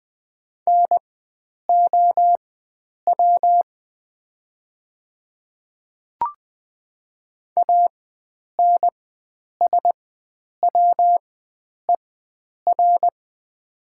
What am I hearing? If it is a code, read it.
Morse code, heard as a keyed tone, about 20 wpm, spaced as for 9 wpm, 700 Hz. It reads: NOW ANSWER
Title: 4x Character Spacing - Top 500 Words - 20wpm